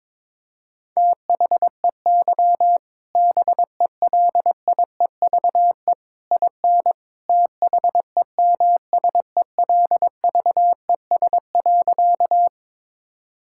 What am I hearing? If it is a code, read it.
Morse code, heard as a keyed tone, about 22 wpm, 700 Hz. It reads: THEY BELIEVE IN THEMSELVES.
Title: Sentences from Top 1000 Words - Review Part 5 - 22wpm